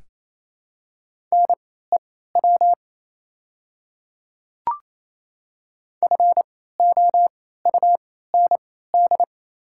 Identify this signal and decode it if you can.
Morse code, heard as a keyed tone, about 27 wpm, spaced as for 16 wpm, 690 Hz. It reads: NEW FOUND
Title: 3x Character Spacing - Top 500 Words - 28wpm